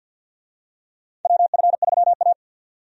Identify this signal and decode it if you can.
Morse code, heard as a keyed tone, about 48 wpm, 700 Hz. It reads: WF3A
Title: USA Call Signs - 4 Characters - 50wpm